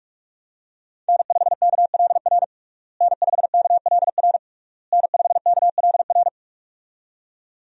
Morse code, heard as a keyed tone, about 46 wpm, 700 Hz. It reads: N5KLR N5KLR N5KLR
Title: USA Call Signs - 5 Characters - Repeated 3x - 45wpm